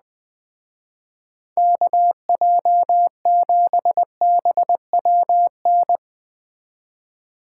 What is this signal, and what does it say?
Morse code, heard as a keyed tone, about 20 wpm, 700 Hz. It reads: KJ7BWN
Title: Speed Racing - USA 2x3 Call Signs 17wpm